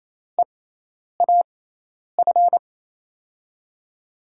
Morse code, heard as a keyed tone, about 28 wpm, spaced as for 10 wpm, 700 Hz. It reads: EAF
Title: USA Call Signs - 6 Characters - 6x Character Spacing - 28wpm